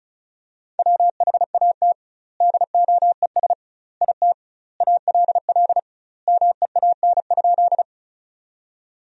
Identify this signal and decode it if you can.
Morse code, heard as a keyed tone, about 36 wpm, 700 Hz. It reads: WHAT DOES IT ALL MEAN?
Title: 2x Word-Spacing - Sentences from Top 200 Words - 35wpm